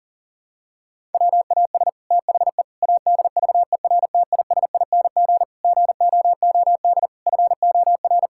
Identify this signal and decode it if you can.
Morse code, heard as a keyed tone, about 40 wpm, 700 Hz. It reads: WAS THE ADVERTISING GOOD FOR
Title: Sentences from Top 1000 Words - Review Part 6 - 40wpm